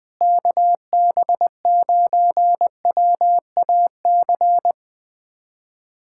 Morse code, heard as a keyed tone, about 21 wpm, 700 Hz. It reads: KB9WAC
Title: USA Callsign DE Callsign 20wpm